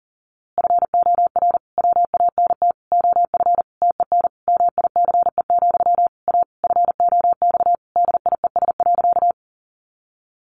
Morse code, heard as a keyed tone, about 40 wpm, 700 Hz. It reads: FOR WANT OF TEN MICE, A FOX DIES.
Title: A Sound Of Thunder 40wpm